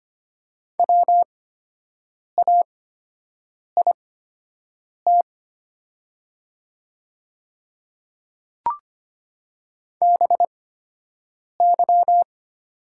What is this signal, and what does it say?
Morse code, heard as a keyed tone, about 25 wpm, spaced as for 7 wpm, 690 Hz. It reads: WAIT BY
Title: 8x Character Spacing - Top 500 Words - 25wpm